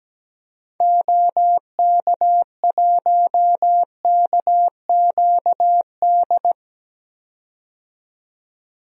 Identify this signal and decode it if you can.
Morse code, heard as a keyed tone, about 17 wpm, 700 Hz. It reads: OK1KQD
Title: DXCC-100 Call Signs - 6 Characters - 17wpm